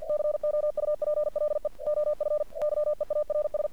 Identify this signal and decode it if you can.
Morse code, heard as a keyed tone, about 49 wpm, 600 Hz. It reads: COUPLE OF YEARS
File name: archived / CRSnet